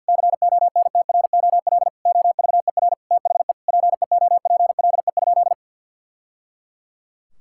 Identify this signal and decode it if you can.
Morse code, heard as a keyed tone, about 50 wpm, 700 Hz. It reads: CONTROL OVER THE PEOPLE?
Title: Sentences from Top 1000 Words - Review Part 5 - 50wpm